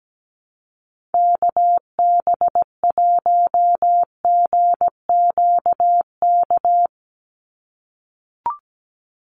Morse code, heard as a keyed tone, about 17 wpm, 700 Hz. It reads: KB1GQK
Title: USA Call Signs 17wpm